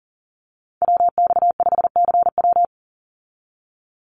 Morse code, heard as a keyed tone, about 40 wpm, 700 Hz. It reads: WX5CW
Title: Call Signs Repeated 3x - 40wpm